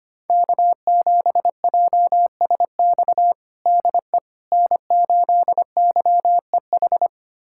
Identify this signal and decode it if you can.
Morse code, heard as a keyed tone, about 25 wpm, 700 Hz. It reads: K7JSX DE N8YEH
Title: USA Callsign DE Callsign - 5 Characters - 25wpm